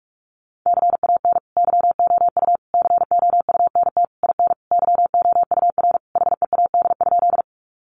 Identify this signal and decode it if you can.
Morse code, heard as a keyed tone, about 44 wpm, 700 Hz. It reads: CAN YOU COUNT IN YOUR HEAD?
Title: Sentences from Top 1500 Words 45wpm